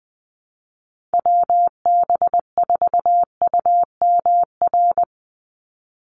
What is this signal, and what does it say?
Morse code, heard as a keyed tone, about 20 wpm, 700 Hz. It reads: WB4UMR
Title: USA Call Signs 20wpm